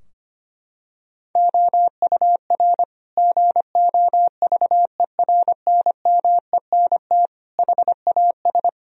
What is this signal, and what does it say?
Morse code, heard as a keyed tone, about 25 wpm, 700 Hz. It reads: OUR GOVERNMENT HAS
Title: Sentences from Top 1000 Words - Review Part 4 - 25wpm